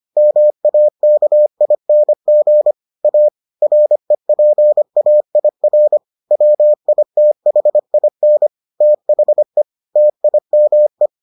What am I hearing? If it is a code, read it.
Morse code, heard as a keyed tone, about 25 wpm, 600 Hz. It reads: MAKING A REPAIR WITHIN THE TIME